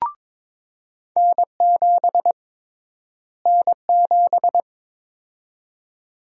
Morse code, heard as a keyed tone, about 22 wpm, 700 Hz. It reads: N7 N7
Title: USA Call Sign Prefixes - Repeated 2x - 22wpm